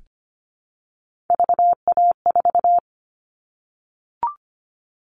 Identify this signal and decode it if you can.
Morse code, heard as a keyed tone, about 25 wpm, 700 Hz. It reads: VA4
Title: All Cut Numbers - 3-Digit 25wpm